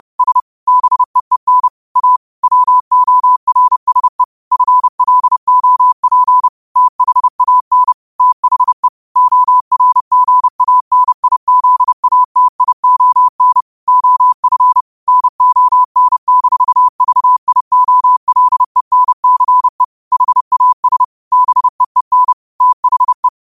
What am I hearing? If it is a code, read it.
Morse code, heard as a keyed tone, about 30 wpm, 1000 Hz. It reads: S BEEN A WORSE FLOP THAN THE ORGANIZATION OF NON-VIOLENCE HAS BEEN THE
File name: 30wpm